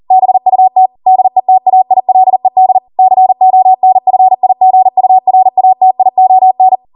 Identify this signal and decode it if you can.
Morse code, heard as a keyed tone, about 40 wpm, 750 Hz. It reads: BUT DETAILED CONFIGURATION